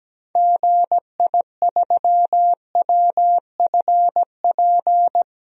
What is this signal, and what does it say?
Morse code, heard as a keyed tone, about 17 wpm, 700 Hz. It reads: GI3WFP
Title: DXCC-100 Call Signs - 6 Characters - 17wpm